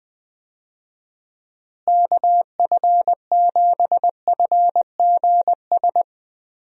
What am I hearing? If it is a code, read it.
Morse code, heard as a keyed tone, about 19 wpm, 700 Hz. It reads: KF7FGS
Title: USA Call Signs - 6 Characters - Repeated 3x - 20wpm